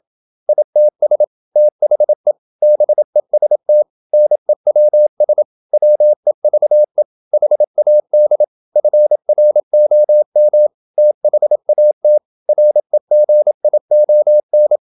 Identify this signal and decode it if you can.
Morse code, heard as a keyed tone, about 28 wpm, 600 Hz. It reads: ITS THE BEST NEWS WEVE HAD FROM THAT REGION
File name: Never 0027wpm